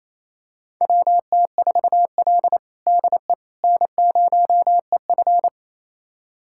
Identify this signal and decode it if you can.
Morse code, heard as a keyed tone, about 28 wpm, 700 Hz. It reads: WT4L DE N0EF
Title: USA Callsign DE Callsign - 4 Characters - 28wpm